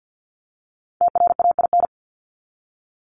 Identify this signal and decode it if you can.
Morse code, heard as a keyed tone, about 50 wpm, 700 Hz. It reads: TRAIN
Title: ING Suffix - Top 100 Words with Base Words 50wpm